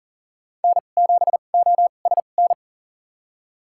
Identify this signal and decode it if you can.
Morse code, heard as a keyed tone, about 40 wpm, spaced as for 29 wpm, 700 Hz. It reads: N7OSN
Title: USA Call Signs - 5 Characters - 2x Character Spacing - 40wpm